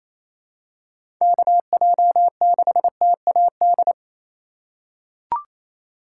Morse code, heard as a keyed tone, about 28 wpm, 700 Hz. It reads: KJ6TAD
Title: USA Call Signs - 6 Characters - 3x Character Spacing - 28wpm